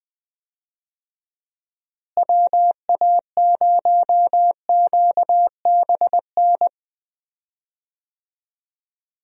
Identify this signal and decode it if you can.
Morse code, heard as a keyed tone, about 21 wpm, 700 Hz. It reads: WA0QBN
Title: USA Call Signs - 6 Characters - Repeated 3x - 20wpm